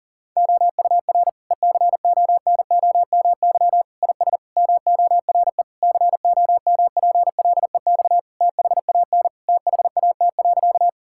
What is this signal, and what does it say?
Morse code, heard as a keyed tone, about 40 wpm, 700 Hz. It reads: OUR ECONOMY IS MORE COMPLEX THAN THAT.
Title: Sentences from Top 1000 Words - Review Part 6 - 40wpm